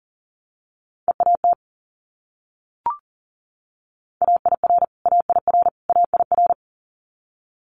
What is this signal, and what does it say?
Morse code, heard as a keyed tone, about 40 wpm, 700 Hz. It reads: EAT AIR AIR AIR
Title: Multi-letter Phonemes Repeated 3x 40wpm